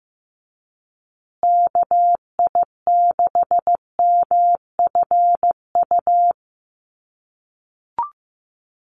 Morse code, heard as a keyed tone, about 15 wpm, 700 Hz. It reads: KI6MFU
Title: USA Call Signs Repeated 3x 15wpm